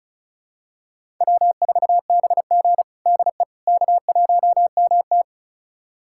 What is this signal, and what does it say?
Morse code, heard as a keyed tone, about 34 wpm, 700 Hz. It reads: W4BG DE K1MT
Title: USA Callsign DE Callsign - 4 Characters - 35wpm